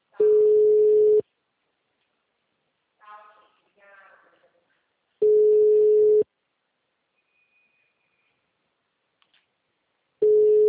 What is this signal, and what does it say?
Ringback tone (Europe, 425 Hz), with speech.
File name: out- -2013- - - .3364